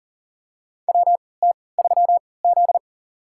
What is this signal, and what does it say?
Morse code, heard as a keyed tone, about 37 wpm, spaced as for 22 wpm, 700 Hz. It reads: WT3Z
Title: USA Call Signs - 4 Characters - 3x Character Spacing - 40wpm